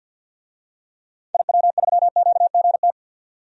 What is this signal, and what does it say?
Morse code, heard as a keyed tone, about 46 wpm, 700 Hz. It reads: IW3QGT
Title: DXCC-100 Call Signs - 6 Characters - 50wpm